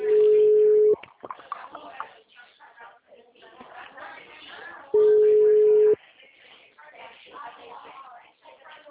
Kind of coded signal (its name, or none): telephone call progress tone